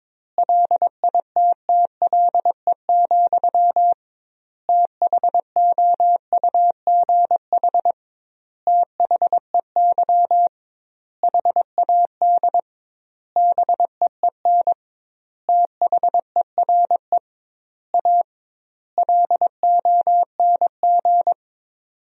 Morse code, heard as a keyed tone, about 22 wpm, 700 Hz. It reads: LITTLE, THOUGH THEY HAD BEEN THERE A LONG
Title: 2x Word-Spacing - Sentences from Top 400 Words - 22wpm